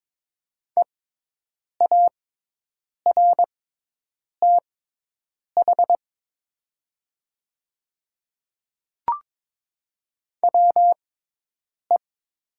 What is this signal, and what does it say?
Morse code, heard as a keyed tone, about 23 wpm, spaced as for 8 wpm, 700 Hz. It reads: EARTH WE
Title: 6x Character Spacing - Top 200 Words - 22wpm